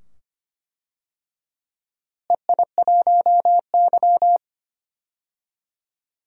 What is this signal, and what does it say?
Morse code, heard as a keyed tone, about 25 wpm, 700 Hz. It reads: EI1Y
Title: DXCC-100 Call Signs - 4 Characters - Repeated 2x - 25wpm